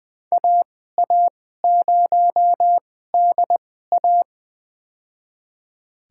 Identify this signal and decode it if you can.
Morse code, heard as a keyed tone, about 21 wpm, spaced as for 15 wpm, 700 Hz. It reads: AA0DA
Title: USA Call Signs - 5 Characters - 2x Character Spacing - 20wpm